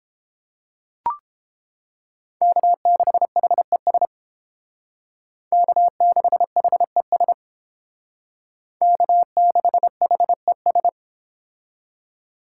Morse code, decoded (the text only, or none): K6HES K6HES K6HES